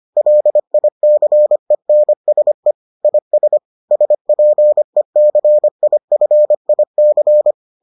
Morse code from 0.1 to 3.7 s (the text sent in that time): LICENSE IS